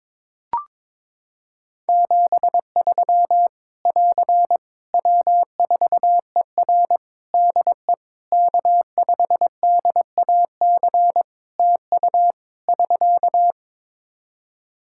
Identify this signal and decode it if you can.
Morse code, heard as a keyed tone, about 23 wpm, 700 Hz. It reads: 73 <AR> W4ER DE K5DAC TU <SK>
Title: K1USN SST QSO Practice 22wpm